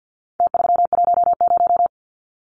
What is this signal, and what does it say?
Morse code, heard as a keyed tone, about 50 wpm, 700 Hz. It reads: T310
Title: Numbers - Contest Style - 4-Digit 50wpm